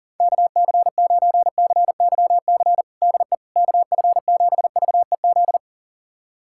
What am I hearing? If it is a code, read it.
Morse code, heard as a keyed tone, about 40 wpm, 700 Hz. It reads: KC9CYC DE KF7VEZ
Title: USA Callsign DE Callsign 40wpm